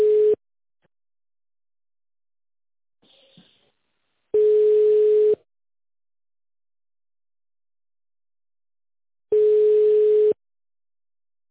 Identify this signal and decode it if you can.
Ringback tone (Europe, 425 Hz).